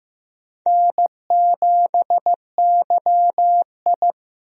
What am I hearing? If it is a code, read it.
Morse code, heard as a keyed tone, about 16 wpm, 700 Hz. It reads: N7YI
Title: USA Call Signs - 4 Characters - Repeated 2x - 15wpm